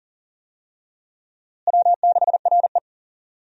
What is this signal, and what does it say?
Morse code, heard as a keyed tone, about 40 wpm, 700 Hz. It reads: W6RE
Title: USA Call Signs - 4 Characters - Repeated 3x - 40wpm